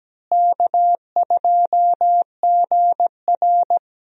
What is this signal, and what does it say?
Morse code, heard as a keyed tone, about 17 wpm, 700 Hz. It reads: K2GR
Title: USA Call Signs - 4 Characters - Repeated 2x - 17wpm